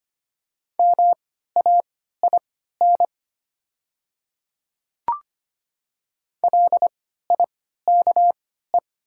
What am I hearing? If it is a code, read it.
Morse code, heard as a keyed tone, about 25 wpm, spaced as for 14 wpm, 700 Hz. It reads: MAIN LIKE
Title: 3x Character Spacing - Top 300 Words - 25wpm